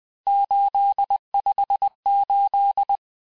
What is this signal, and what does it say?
Morse code, heard as a keyed tone, about 20 wpm, 800 Hz. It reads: 858
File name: cours-858-texte-chiffre-vitesse-19mpm